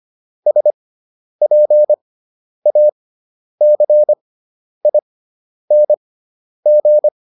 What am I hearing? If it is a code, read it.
Morse code, heard as a keyed tone, about 25 wpm, spaced as for 10 wpm, 600 Hz. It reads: SPACING